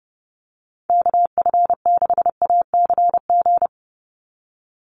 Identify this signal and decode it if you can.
Morse code, heard as a keyed tone, about 30 wpm, 700 Hz. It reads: KF6ACG
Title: USA Call Signs 30wpm